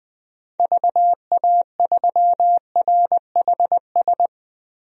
Morse code, heard as a keyed tone, about 20 wpm, 700 Hz. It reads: VA3RHS
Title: DXCC-100 Call Signs - 6 Characters - Repeated 2x - 20wpm